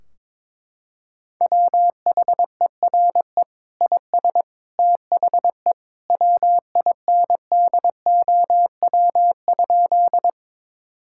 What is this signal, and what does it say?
Morse code, heard as a keyed tone, about 22 wpm, 700 Hz. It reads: WHERE IS THE WINDOW?
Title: Sentences from Top 1600 Words 22wpm